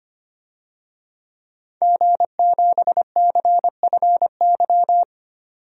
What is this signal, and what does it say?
Morse code, heard as a keyed tone, about 25 wpm, 700 Hz. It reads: G7CFY
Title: DXCC-100 Call Signs - 5 Characters - 25wpm